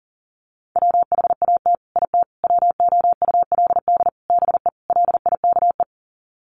Morse code, heard as a keyed tone, about 40 wpm, 700 Hz. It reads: WHAT IT WOULD BE LIKE
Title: Sets of 5 Words Encore 40wpm